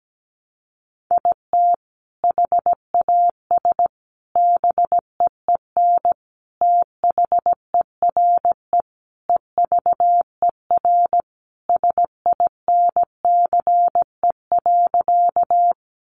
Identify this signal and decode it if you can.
Morse code, heard as a keyed tone, about 17 wpm, 700 Hz. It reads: IT HAS BEEN THERE EVER SINCE.
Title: Sentences from Top 400 Words 17wpm